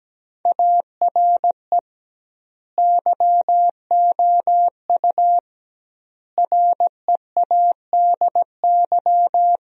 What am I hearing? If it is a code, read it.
Morse code, heard as a keyed tone, about 17 wpm, 700 Hz. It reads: ARE YOU READY
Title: 2x Word-Spacing - Sentences from Top 400 Words - 17wpm